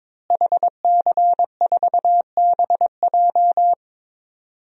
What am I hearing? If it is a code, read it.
Morse code, heard as a keyed tone, about 22 wpm, 700 Hz. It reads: HC4BJ
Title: DXCC-100 Call Signs - 5 Characters - Repeated 2x - 22wpm